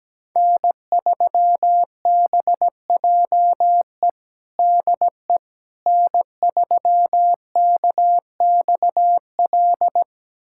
Morse code, heard as a keyed tone, about 17 wpm, 700 Hz. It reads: N3BJE DE N3KXL
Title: USA Callsign DE Callsign - 5 Characters - 17wpm